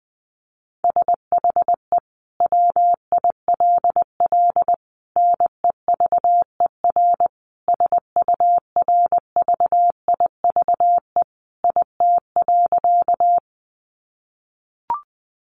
Morse code, heard as a keyed tone, about 20 wpm, 700 Hz. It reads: SHE WILL NEVER SURVIVE IT.
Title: Sentences from Top 1300 Words 20wpm